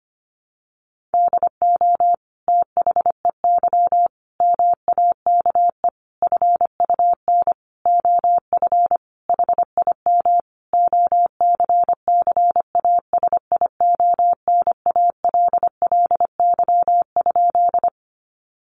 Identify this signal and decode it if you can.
Morse code, heard as a keyed tone, about 25 wpm, 700 Hz. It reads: DO THEY MAKE FUN OF HIM OCCASIONALLY?
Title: Sentences from Top 1300 Words 25wpm